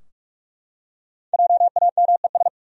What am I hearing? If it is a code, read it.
Morse code, heard as a keyed tone, about 46 wpm, 700 Hz. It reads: JAMES
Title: USA Common Names - 45wpm